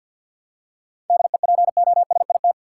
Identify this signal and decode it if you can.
Morse code, heard as a keyed tone, about 50 wpm, 700 Hz. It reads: DEPOSIT
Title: Seven Letter Words - 50wpm